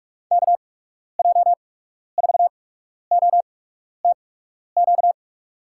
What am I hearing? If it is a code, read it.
Morse code, heard as a keyed tone, about 46 wpm, spaced as for 12 wpm, 700 Hz. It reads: KJ4OTQ
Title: USA Call Signs - 6 Characters - 8x Character Spacing - 45wpm